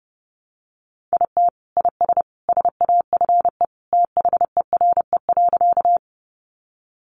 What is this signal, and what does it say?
Morse code, heard as a keyed tone, about 30 wpm, 700 Hz. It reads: IT IS SAFE THERE.
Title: Easy Phrases 30wpm